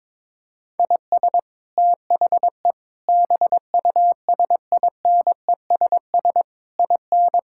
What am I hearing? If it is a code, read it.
Morse code, heard as a keyed tone, about 22 wpm, 700 Hz. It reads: IS THE BUSINESS IN
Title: Sentences from Top 1000 Words - Review Part 5 - 22wpm